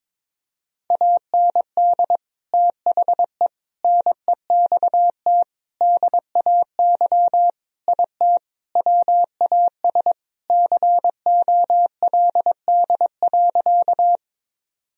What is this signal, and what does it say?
Morse code, heard as a keyed tone, about 22 wpm, 700 Hz. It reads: AND THE NEXT DAY IT WAS COLD.
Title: Sentences from Top 1000 Words - Review Part 1 - 22wpm